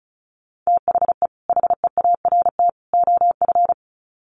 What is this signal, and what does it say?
Morse code, heard as a keyed tone, about 34 wpm, 700 Hz. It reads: THE HEART OF
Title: Sets of 3 Words 35wpm